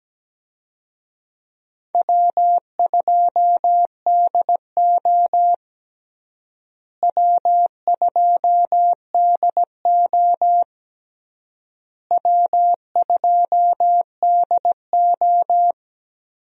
Morse code, heard as a keyed tone, about 17 wpm, 700 Hz. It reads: W2DO W2DO W2DO
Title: USA Call Signs - 4 Characters - Repeated 3x - 17wpm